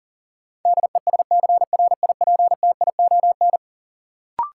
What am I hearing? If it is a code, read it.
Morse code, heard as a keyed tone, about 40 wpm, 700 Hz. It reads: DESCRIPTION
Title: Sentences from Top 1300 Words - New Word Drill Alt - 40wpm